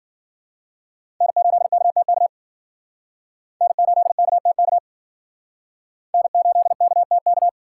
Morse code, heard as a keyed tone, about 52 wpm, 690 Hz. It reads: N8KTK N8KTK N8KTK